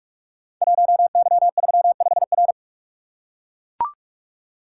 Morse code, heard as a keyed tone, about 44 wpm, 690 Hz. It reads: 1Y35R
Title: Code Groups - 5 Letter:Numbers - 45wpm